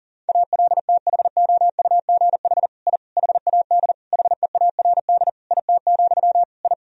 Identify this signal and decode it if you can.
Morse code, heard as a keyed tone, about 40 wpm, 700 Hz. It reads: ALTHOUGH I HAD HEARD IT, I
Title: Sentences from Top 1000 Words - Review Part 3 - 40wpm